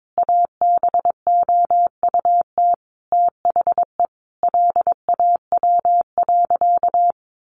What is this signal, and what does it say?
Morse code, heard as a keyed tone, about 22 wpm, 700 Hz. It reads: ABOUT THE LAW.
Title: Sentences from Top 1400 Words 22wpm